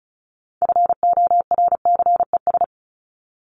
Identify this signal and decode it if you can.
Morse code, heard as a keyed tone, about 34 wpm, 700 Hz. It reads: FORCES
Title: ES:S Suffix - Top 100 Words with Base Words 35wpm